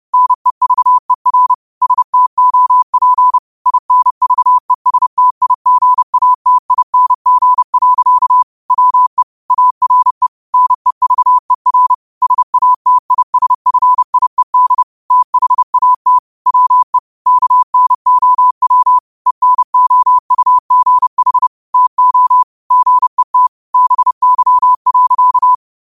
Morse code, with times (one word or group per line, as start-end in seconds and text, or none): 0.1-1.5 NEVER
1.8-3.4 STOP
3.7-8.4 INVESTIGATING.
8.7-9.2 WE
9.5-10.3 ARE
10.5-11.9 NEVER
12.2-14.8 SATISFIED
15.1-16.2 THAT
16.5-17.0 WE
17.3-19.0 KNOW
19.3-21.5 ENOUGH
21.7-22.4 TO
22.7-23.5 GET
23.7-25.5 BY.